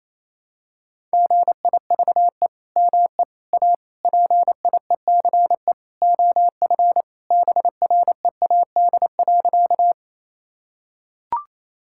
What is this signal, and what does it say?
Morse code, heard as a keyed tone, about 28 wpm, 700 Hz. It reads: GIVE ME A PIECE OF BREAD.
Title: Sentences from Top 1600 Words 28wpm